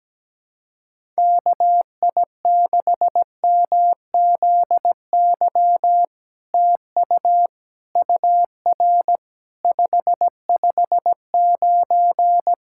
Morse code, heard as a keyed tone, about 17 wpm, 700 Hz. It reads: KI6MZY TU UR 559
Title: SOTA QSO Practice 17wpm